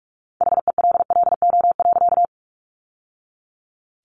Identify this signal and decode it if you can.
Morse code, heard as a keyed tone, about 44 wpm, 700 Hz. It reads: HELLO.
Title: Sentences from Top 1100 Words 45wpm